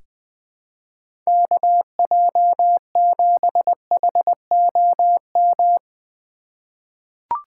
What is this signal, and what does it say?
Morse code, heard as a keyed tone, about 20 wpm, 700 Hz. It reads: KJ7HOM
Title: USA Call Signs - 6x Character Spacing - 20wpm